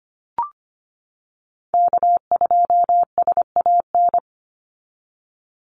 Morse code, heard as a keyed tone, about 25 wpm, 700 Hz. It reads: K2SAN
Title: Call Signs - 25wpm